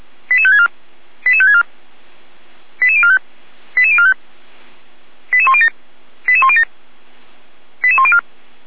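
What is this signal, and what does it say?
Five-tone selective call (ZVEI), heard as quick runs of five tones, tones of about 80 ms. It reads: 88564 88564 88045 88045 88107 88107 88104
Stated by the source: ZVEI five-tone selective call received by SDR, fire brigade Feuerwehr Orlamünde.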